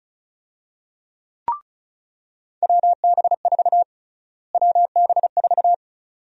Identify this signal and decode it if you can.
Morse code, heard as a keyed tone, about 36 wpm, 700 Hz. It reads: WB4 WB4
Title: USA Call Sign Prefixes - Repeated 2x - 35wpm